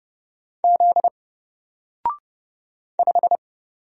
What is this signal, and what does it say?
Morse code, heard as a keyed tone, about 30 wpm, 700 Hz. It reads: Z 5
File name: Single-Letter-Number-30wpm